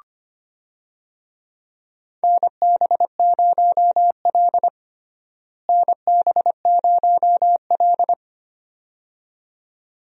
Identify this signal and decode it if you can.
Morse code, heard as a keyed tone, about 25 wpm, 700 Hz. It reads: NB0L NB0L
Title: USA Call Signs - 4 Characters - Repeated 2x - 25wpm